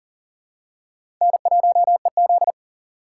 Morse code, heard as a keyed tone, about 39 wpm, 700 Hz. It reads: N1EZ
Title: USA Call Signs - 4 Characters - 4x Character Spacing - 40wpm